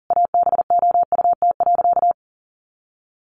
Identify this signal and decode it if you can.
Morse code, heard as a keyed tone, about 37 wpm, 700 Hz. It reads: ABOUT.
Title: Sentences from Top 1100 Words 40wpm